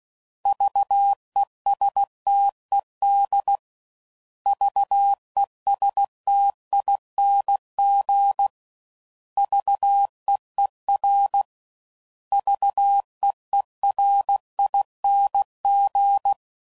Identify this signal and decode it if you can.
Morse code, heard as a keyed tone, about 16 wpm, 800 Hz. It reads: VESTED VESTING VEER VEERING
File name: Prefixes / By First Letter / Vs